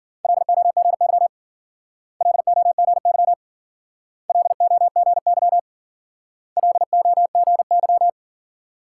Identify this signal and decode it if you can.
Morse code, heard as a keyed tone, about 55 wpm, 700 Hz. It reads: LOGY LOGY LOGY LOGY